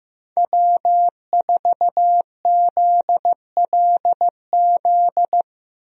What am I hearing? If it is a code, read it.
Morse code, heard as a keyed tone, about 15 wpm, 700 Hz. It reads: W4ZLZ
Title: USA Call Signs - 5 Characters - Repeated 3x - 15wpm